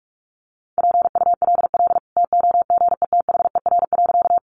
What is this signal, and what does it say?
Morse code, heard as a keyed tone, about 44 wpm, 700 Hz. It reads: PULL TOGETHER.
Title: Sentences from Top 400 Words 45wpm